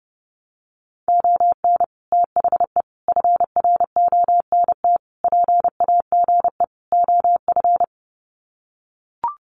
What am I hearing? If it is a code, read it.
Morse code, heard as a keyed tone, about 30 wpm, 700 Hz. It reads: ON THE FRONT PAGE OF
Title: Sets of 5 Words Encore 30wpm